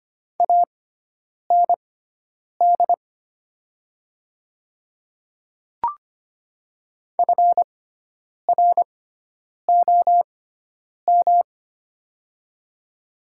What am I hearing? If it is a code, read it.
Morse code, heard as a keyed tone, about 25 wpm, spaced as for 9 wpm, 700 Hz. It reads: AND FROM
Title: 6x Character Spacing - Top 300 Words - 25wpm